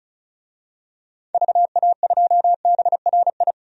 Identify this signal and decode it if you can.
Morse code, heard as a keyed tone, about 36 wpm, 700 Hz. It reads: VA2BRI
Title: DXCC-100 Call Signs - 6 Characters - 35wpm